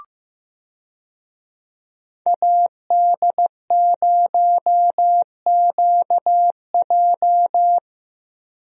Morse code, heard as a keyed tone, about 14 wpm, 700 Hz. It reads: AD0QJ
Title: USA Call Signs - 5 Characters - Repeated 2x - 15wpm